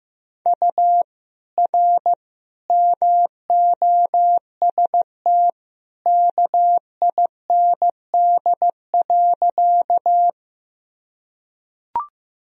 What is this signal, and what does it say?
Morse code, heard as a keyed tone, about 15 wpm, 700 Hz. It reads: U R MOST KIND.
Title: Sentences from Top 200 Words and Abbreviations 15wpm